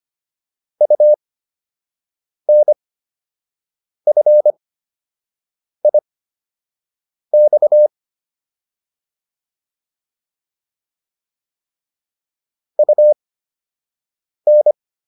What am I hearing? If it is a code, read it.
Morse code, heard as a keyed tone, about 25 wpm, spaced as for 6 wpm, 600 Hz. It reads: UNFIX UN
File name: UN5 06